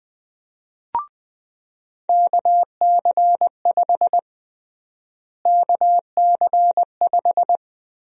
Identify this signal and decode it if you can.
Morse code, heard as a keyed tone, about 20 wpm, 700 Hz. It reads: KC5 KC5
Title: USA Call Sign Prefixes - Repeated 2x - 20wpm